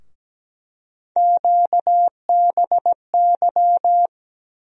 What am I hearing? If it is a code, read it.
Morse code, heard as a keyed tone, about 17 wpm, 700 Hz. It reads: QBY
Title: USA Call Sign Postfixes - 3 Letters 17wpm